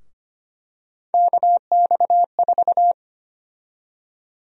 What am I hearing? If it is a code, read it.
Morse code, heard as a keyed tone, about 25 wpm, 700 Hz. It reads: KX4
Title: USA Call Sign Prefixes - 25wpm